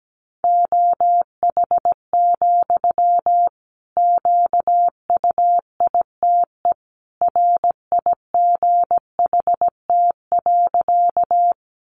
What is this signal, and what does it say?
Morse code, heard as a keyed tone, about 17 wpm, 700 Hz. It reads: OH, QUITE RIGHT.
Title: Sentences from Top 700 Words 17wpm